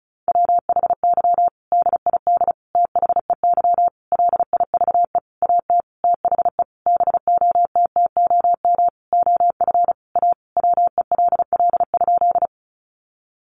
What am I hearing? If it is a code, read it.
Morse code, heard as a keyed tone, about 34 wpm, 700 Hz. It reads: WHY DID THEY LIVE AT THE BOTTOM OF A WELL?
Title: Sentences from Top 900 Words 35wpm